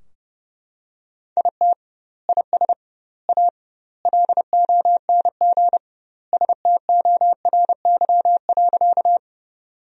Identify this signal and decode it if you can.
Morse code, heard as a keyed tone, about 32 wpm, 700 Hz. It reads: IT IS A LONG STORY.
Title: 2x Word-Spacing - Sentences from Top 300 Words - 30wpm